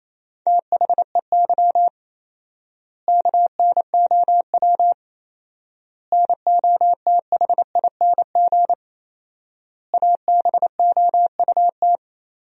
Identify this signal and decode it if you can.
Morse code, heard as a keyed tone, about 28 wpm, 700 Hz. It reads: THEY KNOW NOTHING ABOUT
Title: 4x Word-Spacing - Sentences from Top 500 Words - 28wpm